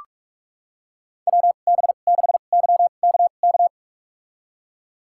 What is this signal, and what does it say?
Morse code, heard as a keyed tone, about 46 wpm, spaced as for 33 wpm, 700 Hz. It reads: WB6YKK
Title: USA Call Signs - 2x Character Spacing - 45wpm